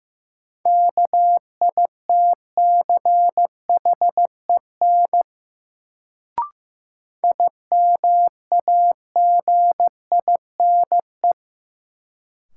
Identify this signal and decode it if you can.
Morse code, heard as a keyed tone, about 16 wpm, 700 Hz. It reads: KITCHEN IMAGINE
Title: Sentences from Top 1400 Words - New Word Drill Alt - 15wpm